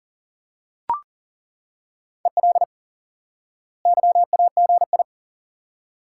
Morse code, heard as a keyed tone, about 40 wpm, 700 Hz. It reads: EL YAGI
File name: IQE Course 3 - Lesson 89 - YAGI - Two Elements - 40wpm